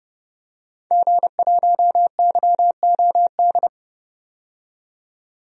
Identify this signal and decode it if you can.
Morse code, heard as a keyed tone, about 32 wpm, 700 Hz. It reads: G1YOD
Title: DXCC-100 Call Signs - 5 Characters - 30wpm